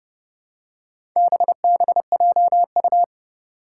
Morse code, heard as a keyed tone, about 30 wpm, 690 Hz. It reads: BBJU
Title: Four Challenging Letter-Number Combinations 30wpm